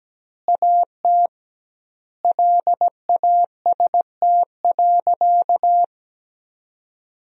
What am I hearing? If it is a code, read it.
Morse code, heard as a keyed tone, about 17 wpm, 700 Hz. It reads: AT LAST.
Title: 2x Word-Spacing - Sentences from Top 300 Words - 17wpm